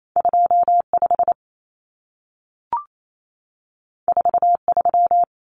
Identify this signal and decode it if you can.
Morse code, heard as a keyed tone, about 28 wpm, 700 Hz. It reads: 25 43
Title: Common Cut Numbers - 2-Digit 28wpm